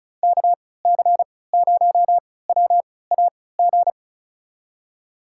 Morse code, heard as a keyed tone, about 34 wpm, spaced as for 20 wpm, 700 Hz. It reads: KC0WAG
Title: USA Call Signs - 6 Characters - 3x Character Spacing - 35wpm